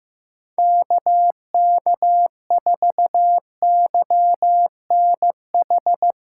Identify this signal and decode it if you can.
Morse code, heard as a keyed tone, about 16 wpm, 700 Hz. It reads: KK4YNH
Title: USA Callsign DE Callsign - 6 Characters - 15wpm